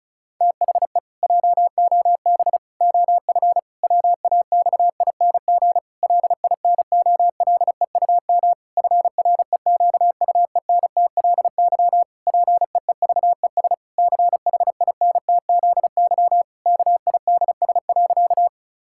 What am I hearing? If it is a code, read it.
Morse code, heard as a keyed tone, about 36 wpm, 700 Hz. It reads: THE JOB OF WAXING LINOLEUM FREQUENTLY PEEVES CHINTZY KIDS.
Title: Pangram Challenge v2 - 35wpm